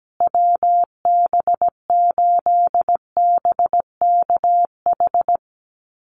Morse code, heard as a keyed tone, about 17 wpm, 700 Hz. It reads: WB8BKH
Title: USA Call Signs Repeated 3x 17wpm